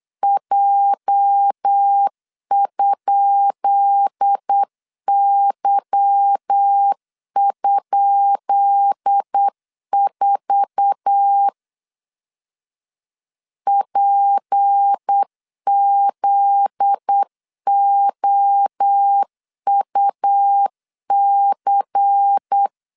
Morse code, heard as a keyed tone, about 8 wpm, 800 Hz. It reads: J?Y?4 PZOUC
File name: LEZIONI / 10 SIGNS / 10B INT 8wpm 3534